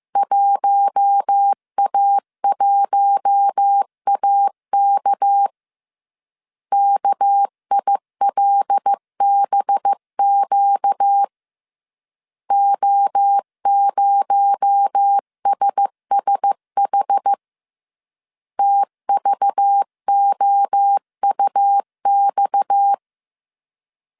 Morse code, heard as a keyed tone, about 15 wpm, 800 Hz. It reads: 1A1AK KILBQ O0SSH TVOUX
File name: CW / DET / 1 6 1500C S C 14wpm 2951